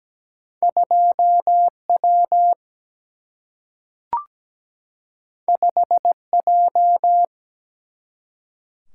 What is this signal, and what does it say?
Morse code, heard as a keyed tone, about 17 wpm, 700 Hz. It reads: 2W 5J
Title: Single-Letter Single-Number 17wpm